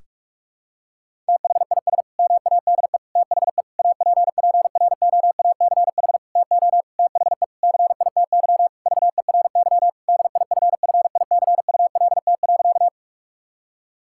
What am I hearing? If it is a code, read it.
Morse code, heard as a keyed tone, about 46 wpm, 700 Hz. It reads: THIS MADE THE APPROACH TO THE CITY VERY DIFFICULT.
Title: Sentences from Top 1000 Words - Review Part 6 - 45wpm